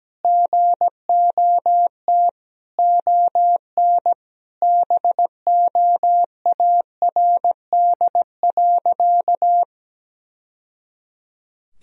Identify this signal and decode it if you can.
Morse code, heard as a keyed tone, about 17 wpm, 700 Hz. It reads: GOT ON BOARD.
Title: Sentences from Top 1000 Words - Review Part 4 - 17wpm